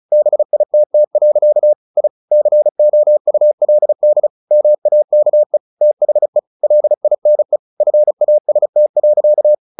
Morse code, heard as a keyed tone, about 34 wpm, 600 Hz. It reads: BITT. I COULD MAKE THE LINE FAST.
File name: the old man and the sea / the old man and the sea 1 35wpm